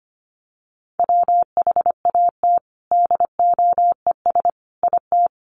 Morse code, heard as a keyed tone, about 25 wpm, 700 Hz. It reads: WHAT DOES IT
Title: Sentences from Top 1400 Words 25wpm